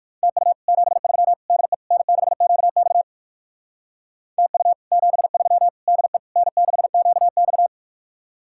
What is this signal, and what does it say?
Morse code, heard as a keyed tone, about 58 wpm, 700 Hz. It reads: TU 73 DE N6QX TU 73 DE N6QX